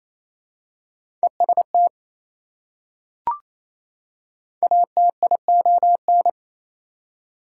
Morse code, heard as a keyed tone, about 28 wpm, 700 Hz. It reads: EST ATION
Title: Prefixes and Suffixes 28wpm